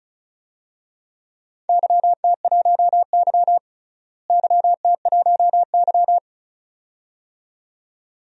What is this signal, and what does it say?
Morse code, heard as a keyed tone, about 34 wpm, 700 Hz. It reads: YT1Y YT1Y
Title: DXCC-100 Call Signs - 4 Characters - Repeated 2x - 35wpm